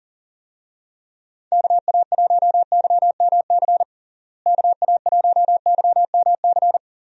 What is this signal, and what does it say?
Morse code, heard as a keyed tone, about 40 wpm, 700 Hz. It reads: KA1YMC KA1YMC
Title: USA Call Signs - 6 Characters - Repeated 3x - 40wpm